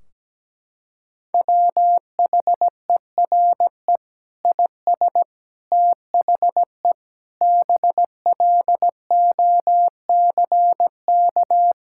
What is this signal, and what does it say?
Morse code, heard as a keyed tone, about 17 wpm, 700 Hz. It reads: WHERE IS THE BLOCK
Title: Sentences from Top 1600 Words 17wpm